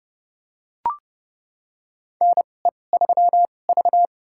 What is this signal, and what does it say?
Morse code, heard as a keyed tone, about 30 wpm, spaced as for 22 wpm, 700 Hz. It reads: NE3V
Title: USA Call Signs - 4 Characters - 2x Character Spacing - 30wpm